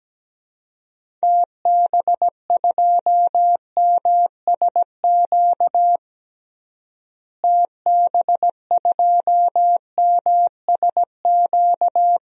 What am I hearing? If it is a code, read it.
Morse code, heard as a keyed tone, about 17 wpm, 700 Hz. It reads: TB2MSQ TB2MSQ
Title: DXCC-100 Call Signs - 6 Characters - Repeated 3x - 17wpm